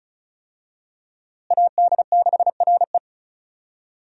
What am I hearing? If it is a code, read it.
Morse code, heard as a keyed tone, about 36 wpm, 700 Hz. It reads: AD6RE
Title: USA Call Signs - 5 Characters - Repeated 3x - 35wpm